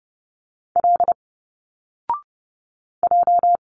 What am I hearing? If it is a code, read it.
Morse code, heard as a keyed tone, about 30 wpm, 700 Hz. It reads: L J
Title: Single Letters 30wpm